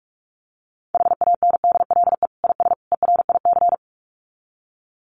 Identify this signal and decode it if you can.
Morse code, heard as a keyed tone, about 44 wpm, 700 Hz. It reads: HANDLE IS ERIC
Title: Common QSO Phrases 45wpm